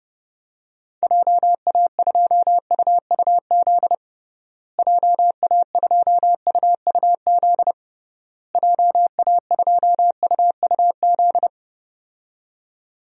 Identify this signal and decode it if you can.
Morse code, heard as a keyed tone, about 30 wpm, 700 Hz. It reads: JA2UUZ JA2UUZ JA2UUZ
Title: DXCC-100 Call Signs - 6 Characters - Repeated 3x - 30wpm